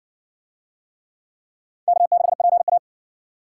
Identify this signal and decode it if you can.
Morse code, heard as a keyed tone, about 60 wpm, 700 Hz. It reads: K6PA